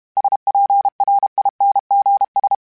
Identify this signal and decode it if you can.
Morse code, heard as a keyed tone, about 32 wpm, 800 Hz. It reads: SPRINGS
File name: fromSMs / FromSMs mp3s / Cities@30